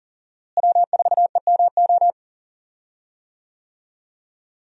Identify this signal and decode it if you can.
Morse code, heard as a keyed tone, about 40 wpm, 700 Hz. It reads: W4EMO
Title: USA Call Signs - 5 Characters - Repeated 2x - 40wpm